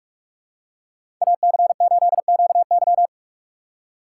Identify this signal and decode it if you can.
Morse code, heard as a keyed tone, about 46 wpm, 700 Hz. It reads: AC8QY
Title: USA Call Signs - 5 Characters - 2x Character Spacing - 45wpm